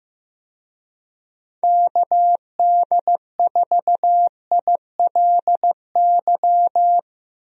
Morse code, heard as a keyed tone, about 15 wpm, 700 Hz. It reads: KD4ILY
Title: USA Call Signs - 6 Characters - 15wpm